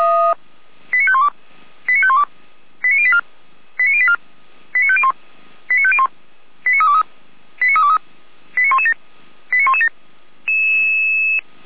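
Five-tone selective call (ZVEI), heard as quick runs of five tones, tones of about 70 ms. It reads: 89512 89512 89084 89084 89501 89501 89323 89323 89117 89117
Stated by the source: ZVEI five-tone selective call received on SDR.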